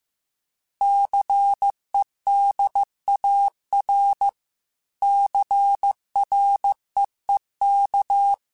Morse code, heard as a keyed tone, about 15 wpm, 800 Hz. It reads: CEDAR CREEK